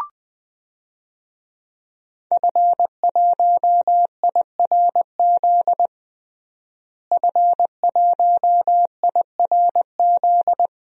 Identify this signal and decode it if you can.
Morse code, heard as a keyed tone, about 20 wpm, 700 Hz. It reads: F1IRZ F1IRZ
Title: DXCC-100 Call Signs - 5 Characters - Repeated 3x - 20wpm